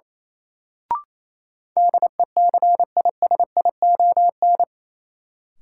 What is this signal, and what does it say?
Morse code, heard as a keyed tone, about 28 wpm, 700 Hz. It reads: DECISION
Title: Sentences from Top 1000 Words - New Word Drill - 28wpm